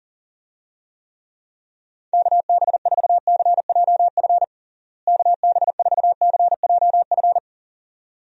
Morse code, heard as a keyed tone, about 40 wpm, 700 Hz. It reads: KB4CJF KB4CJF
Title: USA Call Signs - 6 Characters - Repeated 2x - 40wpm